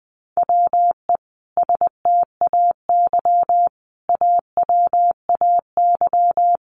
Morse code, heard as a keyed tone, about 20 wpm, 700 Hz. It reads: WE STAY AWAY
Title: A Sound Of Thunder 20wpm